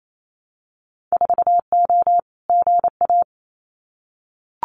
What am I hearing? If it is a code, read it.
Morse code, heard as a keyed tone, about 28 wpm, 700 Hz. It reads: 4O GA
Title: Winter Field Day Exchanges 28wpm